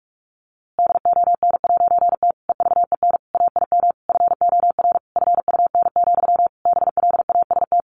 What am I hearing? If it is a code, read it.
Morse code, heard as a keyed tone, about 44 wpm, 700 Hz. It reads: DON'T EVEN AIM FOR FUN, BLAST
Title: A Sound Of Thunder 45wpm